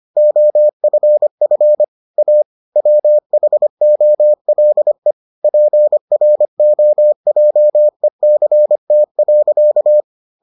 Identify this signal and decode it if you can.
Morse code, heard as a keyed tone, about 25 wpm, 600 Hz. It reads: OFF A WHOLE PROJECT.